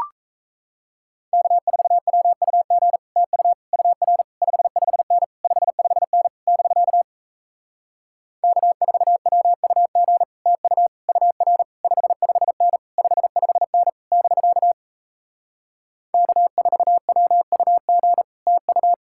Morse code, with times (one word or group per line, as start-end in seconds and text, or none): 1.3-3.0 K4WUG
3.2-3.5 TU
3.7-4.2 UR
4.4-5.2 55N
5.4-6.3 55N
6.5-7.0 <BK>
8.4-10.2 K4WUG
10.5-10.9 TU
11.1-11.6 UR
11.8-12.8 55N
13.0-13.9 55N
14.1-14.7 <BK>
16.1-18.2 K4WUG
18.5-18.9 TU